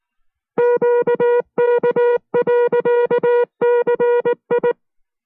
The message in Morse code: QK.CI